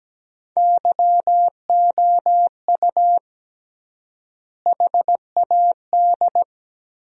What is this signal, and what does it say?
Morse code, heard as a keyed tone, about 17 wpm, 700 Hz. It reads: YOU HAD
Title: 3x Word-Spacing - Sentences from Top 400 Words - 17wpm